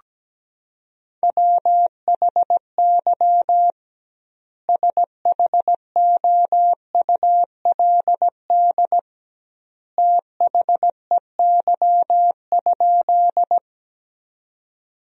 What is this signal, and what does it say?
Morse code, heard as a keyed tone, about 17 wpm, 700 Hz. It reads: WHY SHOULD THEY?
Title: 2x Word-Spacing - Sentences from Top 300 Words - 17wpm